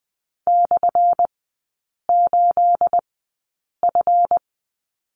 Morse code, heard as a keyed tone, about 19 wpm, spaced as for 8 wpm, 700 Hz. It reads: /8F
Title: 212 Lesson 34e - Number 8 - Three Characters 20wpm